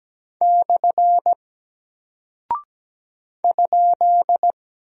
Morse code, heard as a keyed tone, about 17 wpm, 700 Hz. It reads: / ?
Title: Single Character (Letter:Number:Punctuation:Prosign) 17wpm